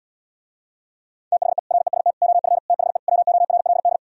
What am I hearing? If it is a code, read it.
Morse code, heard as a keyed tone, about 77 wpm, 700 Hz. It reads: THE REST OF THE WORLD